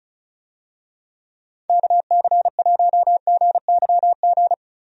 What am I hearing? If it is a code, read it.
Morse code, heard as a keyed tone, about 36 wpm, 700 Hz. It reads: KC1GYG
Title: USA Call Signs - 6 Characters - Repeated 2x - 35wpm